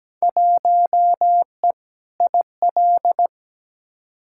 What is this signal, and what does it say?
Morse code, heard as a keyed tone, about 17 wpm, 700 Hz. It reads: 1E IL
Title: Field Day Exchanges - 17wpm